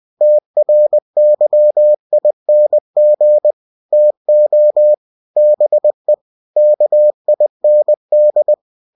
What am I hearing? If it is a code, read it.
Morse code, heard as a keyed tone, about 20 wpm, 600 Hz. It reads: TRYING TO BE KIND
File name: cw-books / winnie-the-pooh / winnie-20wpm-page-0039